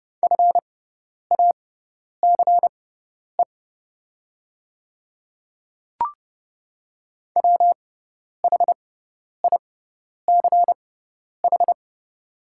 Morse code, heard as a keyed tone, about 30 wpm, spaced as for 10 wpm, 700 Hz. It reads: FACE WHICH
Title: 6x Character Spacing - Top 300 Words - 30wpm